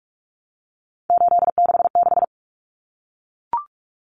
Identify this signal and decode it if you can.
Morse code, heard as a keyed tone, about 44 wpm, 700 Hz. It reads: 866
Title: Common Cut Numbers - 3-Digit 45wpm